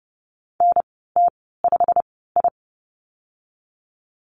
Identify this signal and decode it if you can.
Morse code, heard as a keyed tone, about 30 wpm, spaced as for 17 wpm, 700 Hz. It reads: NT5I
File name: Lesson 15g - Number 5 - Call Signs 30wpm